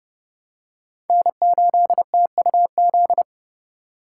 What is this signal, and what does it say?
Morse code, heard as a keyed tone, about 30 wpm, 700 Hz. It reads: N8TUZ
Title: USA Call Signs - 5 Characters - 3x Character Spacing - 30wpm